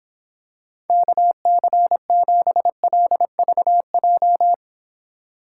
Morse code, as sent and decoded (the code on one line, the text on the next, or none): -.- -.-. --... .-.. ...- .---
KC7LVJ